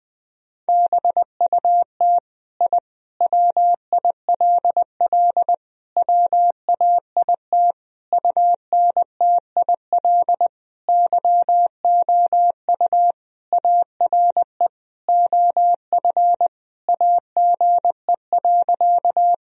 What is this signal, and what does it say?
Morse code, heard as a keyed tone, about 20 wpm, 700 Hz. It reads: BUT I WILL WAIT UNTIL YOU ARE OF AGE.
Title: Sentences from Top 1000 Words - Review Part 3 - 20wpm